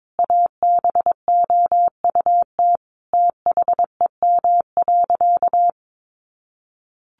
Morse code, heard as a keyed tone, about 22 wpm, 700 Hz. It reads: ABOUT THEM.
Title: Sentences from Top 1500 Words 22wpm